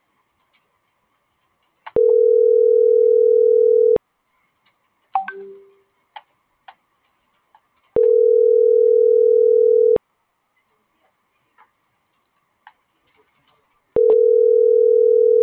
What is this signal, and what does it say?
Ringback tone (North America), 440 + 480 Hz.